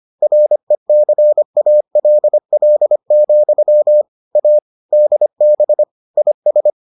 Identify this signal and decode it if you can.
Morse code, heard as a keyed tone, about 25 wpm, 600 Hz. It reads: RECALL, A DB IS